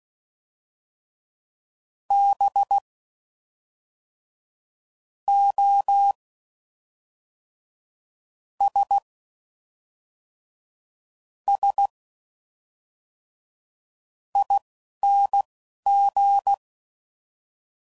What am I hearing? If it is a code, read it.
Morse code, heard as a keyed tone, about 16 wpm, spaced as for 11 wpm, 800 Hz. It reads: B O S S ING
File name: Suffixes / Special INGs / Fast INGs